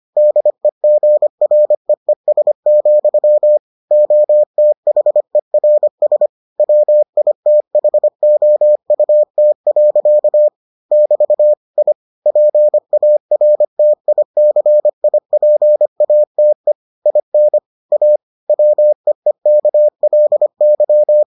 Morse code, heard as a keyed tone, about 25 wpm, 600 Hz. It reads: DEGREES, OTHERS WITHOUT. <BT> I PARTICIPATE IN A WEEKLY